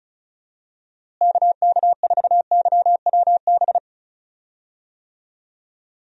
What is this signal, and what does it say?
Morse code, heard as a keyed tone, about 36 wpm, 700 Hz. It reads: KK4YWB
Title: USA Call Signs - 6 Characters - Repeated 3x - 35wpm